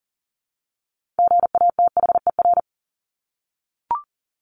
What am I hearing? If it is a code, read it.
Morse code, heard as a keyed tone, about 40 wpm, 700 Hz. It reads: GATHER
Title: Top 1501-1750 Words 40wpm